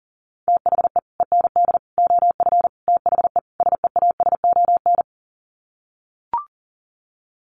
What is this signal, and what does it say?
Morse code, heard as a keyed tone, about 40 wpm, 700 Hz. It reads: THE END OF THE SEASON
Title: Sets of 5 Words Encore 40wpm